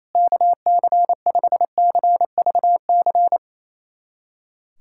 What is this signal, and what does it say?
Morse code, heard as a keyed tone, about 28 wpm, 700 Hz. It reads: KC5CVC
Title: USA Callsign DE Callsign 28wpm